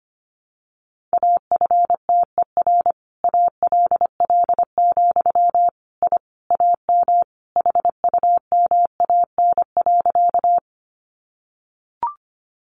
Morse code, heard as a keyed tone, about 25 wpm, 700 Hz. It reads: AFTER ALL, I AM HUMAN.
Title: Sentences from Top 600 Words 25wpm